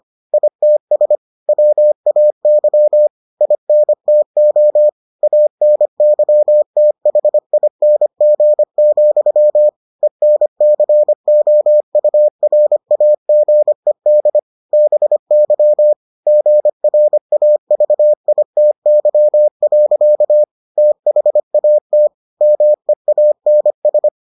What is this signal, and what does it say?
Morse code, heard as a keyed tone, about 25 wpm, 600 Hz. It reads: ITS WAY INTO ANYTHING, ENCOURAGED BY GRAVITY. THAT MEANS